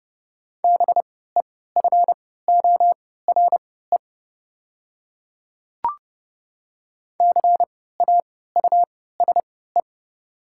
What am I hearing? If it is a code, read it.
Morse code, heard as a keyed tone, about 30 wpm, spaced as for 17 wpm, 700 Hz. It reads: BEFORE CAUSE
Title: 3x Character Spacing - Top 200 Words - 30wpm